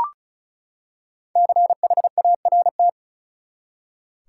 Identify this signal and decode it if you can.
Morse code, heard as a keyed tone, about 34 wpm, 700 Hz. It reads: CHART
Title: Five Letter Words - 35wpm